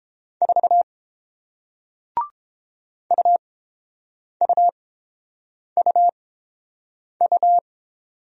Morse code, heard as a keyed tone, about 30 wpm, spaced as for 7 wpm, 700 Hz. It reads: 4 UUUU